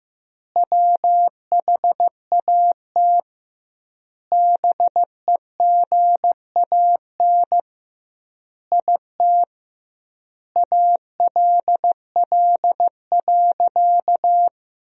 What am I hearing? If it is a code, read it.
Morse code, heard as a keyed tone, about 15 wpm, 700 Hz. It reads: WHAT BEGAN IT ALL.
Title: 2x Word-Spacing - Sentences from Top 300 Words - 15wpm